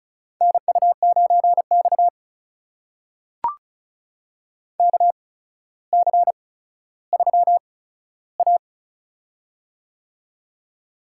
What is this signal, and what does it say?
Morse code, heard as a keyed tone, about 36 wpm, spaced as for 15 wpm, 700 Hz. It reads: NU9X KC3A
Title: USA Call Signs - 8x Character Spacing - 35wpm